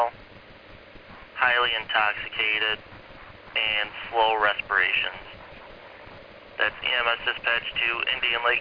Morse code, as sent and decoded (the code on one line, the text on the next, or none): none